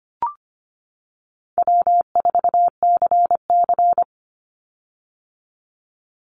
Morse code, heard as a keyed tone, about 25 wpm, 700 Hz. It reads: W4CC
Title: Easy Call Signs - 25wpm